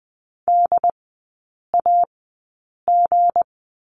Morse code, heard as a keyed tone, about 20 wpm, spaced as for 8 wpm, 700 Hz. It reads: DAG
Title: 198 Lesson 32e - Letter k - Three Characters 20wpm